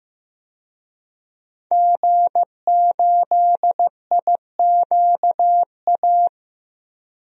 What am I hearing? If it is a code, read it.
Morse code, heard as a keyed tone, about 15 wpm, 700 Hz. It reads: G8IQA
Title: DXCC-100 Call Signs - 5 Characters - 15wpm